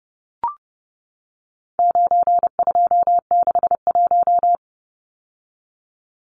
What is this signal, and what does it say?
Morse code, heard as a keyed tone, about 30 wpm, 700 Hz. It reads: 9261
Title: Numbers - 4-Digit 30wpm